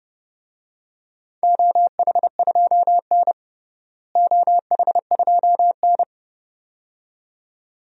Morse code, heard as a keyed tone, about 30 wpm, 700 Hz. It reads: OH2N OH2N
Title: DXCC-100 Call Signs - 4 Characters - Repeated 2x - 30wpm